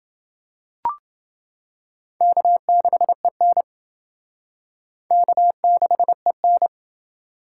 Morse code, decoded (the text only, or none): K6EN K6EN